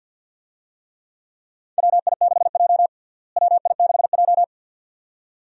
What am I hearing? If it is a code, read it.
Morse code, heard as a keyed tone, about 53 wpm, 700 Hz. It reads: WI6J WI6J
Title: USA Call Signs - 4 Characters - Repeated 2x - 50wpm